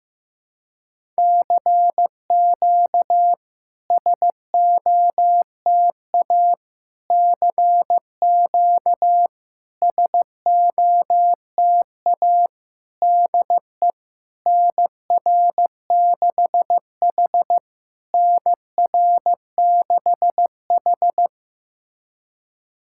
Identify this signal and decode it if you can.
Morse code, heard as a keyed tone, about 16 wpm, 700 Hz. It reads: CQ SOTA CQ SOTA DE NR6H NR6H
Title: SOTA QSO Practice 15wpm